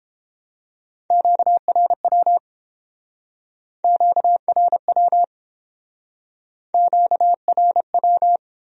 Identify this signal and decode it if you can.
Morse code, heard as a keyed tone, about 28 wpm, 700 Hz. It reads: QRW QRW QRW